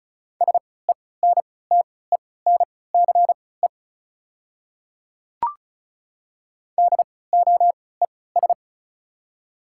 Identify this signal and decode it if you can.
Morse code, heard as a keyed tone, about 36 wpm, spaced as for 20 wpm, 700 Hz. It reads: SENTENCE DOES
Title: 3x Character Spacing - Top 200 Words - 35wpm